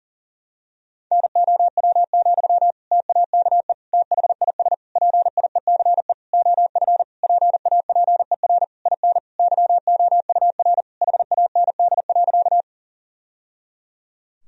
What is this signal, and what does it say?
Morse code, heard as a keyed tone, about 40 wpm, 700 Hz. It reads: NOW, TAKE THIS PIECE OF PAPER IN YOUR HAND.
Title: Sentences from Top 1000 Words - Review Part 2 - 40wpm